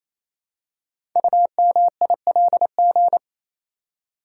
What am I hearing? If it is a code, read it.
Morse code, heard as a keyed tone, about 29 wpm, 700 Hz. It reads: UMILG
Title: Code Groups - 5 Letters - 28wpm